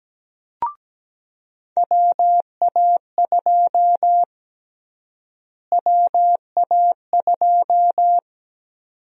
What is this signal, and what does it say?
Morse code, heard as a keyed tone, about 17 wpm, 700 Hz. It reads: WA2 WA2
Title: USA Call Sign Prefixes - Repeated 2x - 17wpm